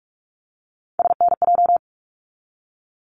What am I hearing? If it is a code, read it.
Morse code, heard as a keyed tone, about 46 wpm, 700 Hz. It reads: SNJ
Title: ARRL Sections 45wpm